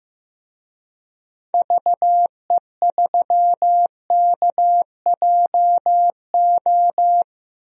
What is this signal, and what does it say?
Morse code, heard as a keyed tone, about 15 wpm, 700 Hz. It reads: VE3KJO
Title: DXCC-100 Call Signs - 6 Characters - Repeated 2x - 15wpm